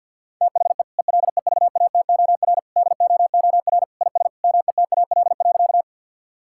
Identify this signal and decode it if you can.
Morse code, heard as a keyed tone, about 52 wpm, 700 Hz. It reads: THE ELEVATOR DOOR IS METAL.
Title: Sentences from Top 1600 Words 50wpm